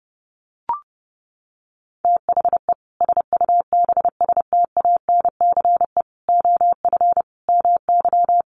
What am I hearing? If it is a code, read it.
Morse code, heard as a keyed tone, about 28 wpm, 700 Hz. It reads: THE SUBSTANCE OF MY
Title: Sentences from Top 1500 Words 30wpm